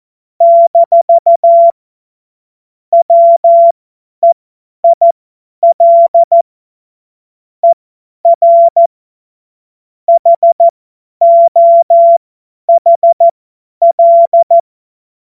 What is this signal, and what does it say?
Morse code, heard as a keyed tone, about 14 wpm, spaced as for 10 wpm, 680 Hz. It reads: - WEIL ER HOHL
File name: Morsekurs / Witze / Witz 16. November 2021, 50 F70